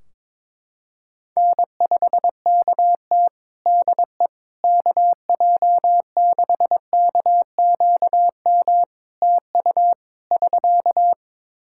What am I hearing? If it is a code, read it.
Morse code, heard as a keyed tone, about 22 wpm, 700 Hz. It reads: N5KT DE KJ6KQM TU <SK>
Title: Practice QSO - Exchange 4 - Ending - 22wpm